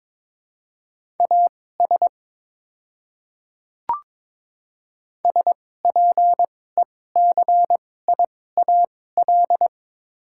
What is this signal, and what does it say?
Morse code, heard as a keyed tone, about 23 wpm, spaced as for 16 wpm, 700 Hz. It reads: AS SPECIAL
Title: 2x Character Spacing - Top 500 Words - 22wpm